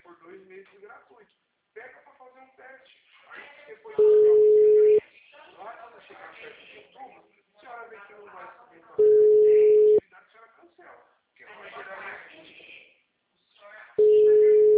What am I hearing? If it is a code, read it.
Ringback tone (Europe, 425 Hz), with speech.
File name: out- -2005- - - .89758